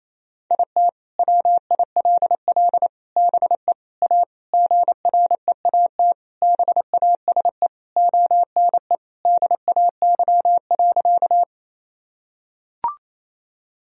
Morse code, heard as a keyed tone, about 28 wpm, 700 Hz. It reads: IT WILL BE A GREAT BASE ONE DAY.
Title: Sentences from Top 1000 Words - Review Part 1 - 28wpm